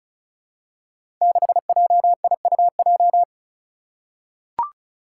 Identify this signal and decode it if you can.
Morse code, heard as a keyed tone, about 36 wpm, 700 Hz. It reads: 6JIUJ
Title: Five Challenging Letter-Number Combinations 35wpm